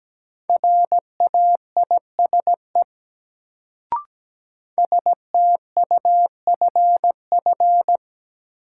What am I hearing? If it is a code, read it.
Morse code, heard as a keyed tone, about 17 wpm, 700 Hz. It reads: RAISE STUFF
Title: Sentences from Top 1400 Words - New Word Drill Alt - 17wpm